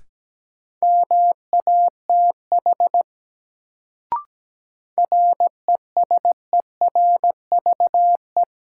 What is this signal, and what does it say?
Morse code, heard as a keyed tone, about 17 wpm, 700 Hz. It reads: MATH RESERVE
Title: Sentences from Top 1000 Words - Review Part 6 - New Word Drill - 17wpm